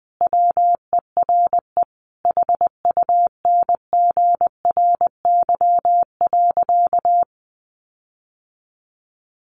Morse code, heard as a keyed tone, about 20 wpm, 700 Hz. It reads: WERE HUNGRY.
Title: Sentences from Top 1400 Words 20wpm